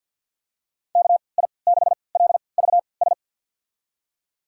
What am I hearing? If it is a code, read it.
Morse code, heard as a keyed tone, about 51 wpm, spaced as for 28 wpm, 700 Hz. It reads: KI6LVS
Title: USA Call Signs - 3x Character Spacing - 50wpm